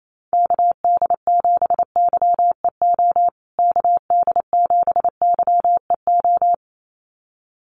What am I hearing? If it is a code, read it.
Morse code, heard as a keyed tone, about 28 wpm, 700 Hz. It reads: KD7YEO KD7YEO
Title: USA Call Signs Repeated 2x 28wpm